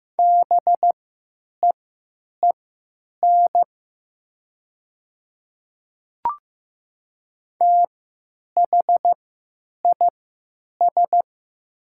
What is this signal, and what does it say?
Morse code, heard as a keyed tone, about 15 wpm, spaced as for 9 wpm, 700 Hz. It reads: BEEN THIS
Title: 3x Character Spacing - Top 500 Words - 15wpm